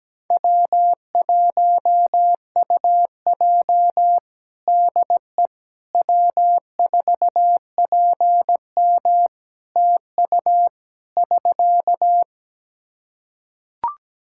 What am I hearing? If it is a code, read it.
Morse code, heard as a keyed tone, about 17 wpm, 700 Hz. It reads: W1UJ DE W4PM TU <SK>
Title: K1USN SST QSO Practice 17wpm